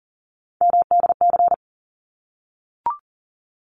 Morse code, heard as a keyed tone, about 40 wpm, 700 Hz. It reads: MDC
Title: ARRL Sections 40wpm